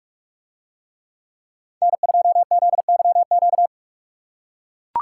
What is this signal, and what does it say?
Morse code, heard as a keyed tone, about 42 wpm, 700 Hz. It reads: N2ZYQ